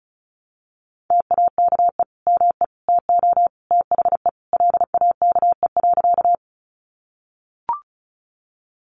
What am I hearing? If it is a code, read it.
Morse code, heard as a keyed tone, about 34 wpm, 700 Hz. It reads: TAKE ME TO THE LAKE.
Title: Sentences from Top 1100 Words 35wpm